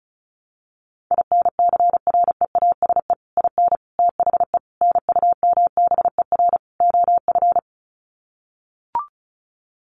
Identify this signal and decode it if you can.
Morse code, heard as a keyed tone, about 34 wpm, 700 Hz. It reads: INCREASE IN THE NUMBER OF
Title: Sets of 5 Words 35wpm